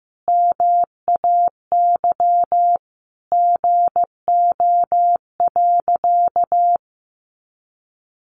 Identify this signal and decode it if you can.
Morse code, heard as a keyed tone, about 15 wpm, 700 Hz. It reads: MAY GO.
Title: Sentences from Top 100 Words 15wpm